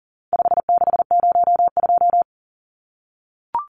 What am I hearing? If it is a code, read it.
Morse code, heard as a keyed tone, about 38 wpm, 700 Hz. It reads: 5602
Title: Numbers - 4-Digit 40wpm